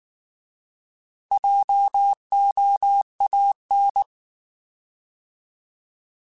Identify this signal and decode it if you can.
Morse code, heard as a keyed tone, about 19 wpm, 800 Hz. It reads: JOAN
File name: Alphabetical / Alphabetical mp3s / Js @18